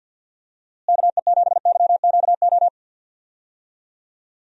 Morse code, heard as a keyed tone, about 46 wpm, 700 Hz. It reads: KE7YQO
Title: USA Call Signs - 6 Characters - Repeated 2x - 50wpm